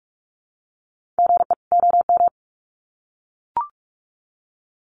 Morse code, heard as a keyed tone, about 44 wpm, 700 Hz. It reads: GE OM
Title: Common QSO Phrases 45wpm